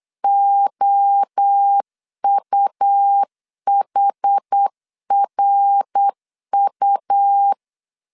Morse code, heard as a keyed tone, about 8 wpm, 800 Hz. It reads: OUHRU
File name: CW / DET / 1 3 1500C S B 8wpm 4141